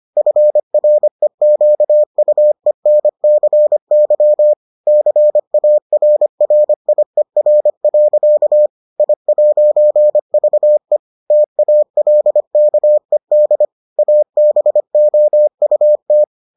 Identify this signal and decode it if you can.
Morse code, heard as a keyed tone, about 25 wpm, 600 Hz. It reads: FREQUENCY CARRIER. I'VE TALKED ABOUT